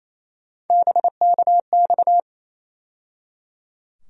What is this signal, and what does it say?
Morse code, heard as a keyed tone, about 28 wpm, 700 Hz. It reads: BKX
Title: USA Call Sign Postfixes - 3 Letters 28wpm